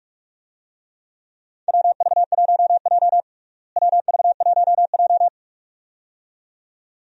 Morse code, heard as a keyed tone, about 44 wpm, 700 Hz. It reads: WV1J WV1J
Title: USA Call Signs - 4 Characters - Repeated 2x - 45wpm